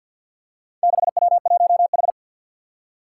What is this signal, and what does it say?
Morse code, heard as a keyed tone, about 52 wpm, 700 Hz. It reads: 6W1H
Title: Four Challenging Letter-Number Combinations 50wpm